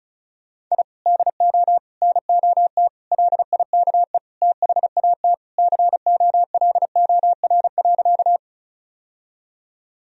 Morse code, heard as a keyed tone, about 36 wpm, 700 Hz. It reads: I DO NOT LIKE THAT COLOR.
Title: Sentences from Top 1000 Words - Review Part 2 - 35wpm